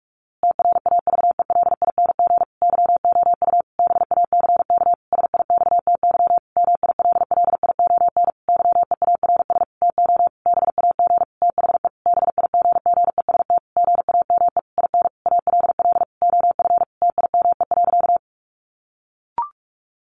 Morse code, heard as a keyed tone, about 44 wpm, 700 Hz. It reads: TRAVELING YOU BACK SIXTY MILLION YEARS TO BAG THE BIGGEST GAME IN ALL OF TIME.
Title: A Sound Of Thunder 45wpm